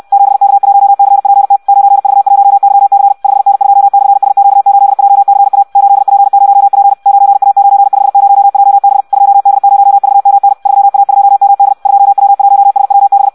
Morse code, heard as a keyed tone, about 64 wpm, 800 Hz. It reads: 7K1CPT 9V1YC 5N2/SQ8JCA 8P0P 9A950JB 2D0VMN 3A2MD 2W1SWB
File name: Audio practica CW / qrq-tx-66 2